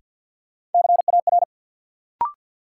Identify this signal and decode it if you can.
Morse code, heard as a keyed tone, about 52 wpm, 700 Hz. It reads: CAR
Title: Sets of 5 Words Encore - New Word Drill - 50wpm